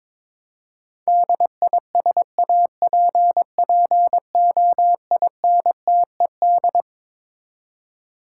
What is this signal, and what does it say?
Morse code, heard as a keyed tone, about 22 wpm, 700 Hz. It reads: DISAPPOINTED
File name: Twelve Letter Words - 22wpm